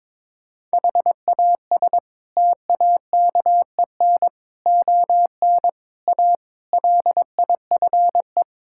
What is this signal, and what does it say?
Morse code, heard as a keyed tone, about 22 wpm, 700 Hz. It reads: HAS TAKEN ON A LIFE
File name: practice / 22wpm / Sets of 8 Words - 22wpm